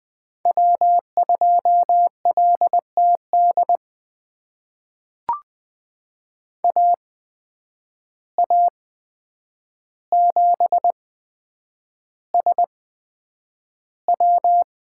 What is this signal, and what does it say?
Morse code, heard as a keyed tone, about 20 wpm, spaced as for 9 wpm, 700 Hz. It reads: W2LTD AA7SW
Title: USA Call Signs - 5 Characters - 8x Character Spacing - 20wpm